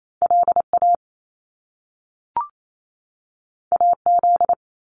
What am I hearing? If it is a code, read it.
Morse code, heard as a keyed tone, about 28 wpm, 700 Hz. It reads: LA AZ
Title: US States 28wpm